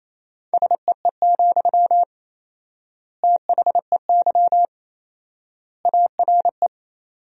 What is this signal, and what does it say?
Morse code, heard as a keyed tone, about 28 wpm, 700 Hz. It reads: SEE, THEY ARE
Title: 4x Word-Spacing - Sentences from Top 200 Words - 28wpm